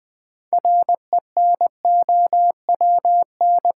Morse code, heard as a keyed tone, about 20 wpm, 700 Hz. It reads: RENOWN
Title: Build A Word - 2 Letters At A Time - 8-Letter Words - 20wpm